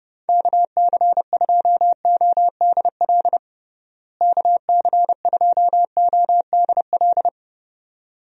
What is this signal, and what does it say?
Morse code, heard as a keyed tone, about 30 wpm, 700 Hz. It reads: KC2ODL KC2ODL
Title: USA Call Signs - 6 Characters - Repeated 2x - 30wpm